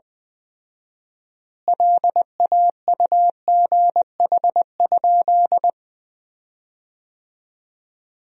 Morse code, heard as a keyed tone, about 20 wpm, 700 Hz. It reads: LAUGH?
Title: 4x Word-Spacing - Sentences from Top 500 Words - 20wpm